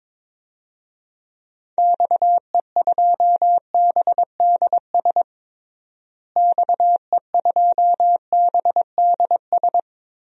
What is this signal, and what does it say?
Morse code, heard as a keyed tone, about 21 wpm, 700 Hz. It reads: XE2BDS XE2BDS
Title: DXCC-100 Call Signs - 6 Characters - Repeated 3x - 22wpm